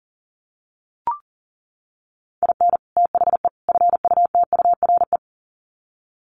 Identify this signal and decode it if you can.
Morse code, heard as a keyed tone, about 40 wpm, 700 Hz. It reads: IN THE FUTURE
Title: Sets of 3 Words 40wpm